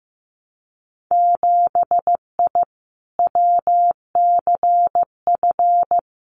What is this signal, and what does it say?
Morse code, heard as a keyed tone, about 15 wpm, 700 Hz. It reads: 7I WCF
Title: Winter Field Day Exchanges Encore 15wpm